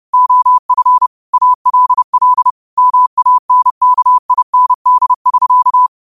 Morse code, heard as a keyed tone, about 30 wpm, 1000 Hz. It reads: OF ALL MANKIND <SK>
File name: qotdfiles / 30wpm